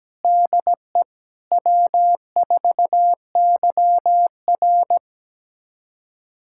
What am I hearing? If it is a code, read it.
Morse code, heard as a keyed tone, about 17 wpm, 700 Hz. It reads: DE W4YR
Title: USA Callsign DE Callsign - 4 Characters - 17wpm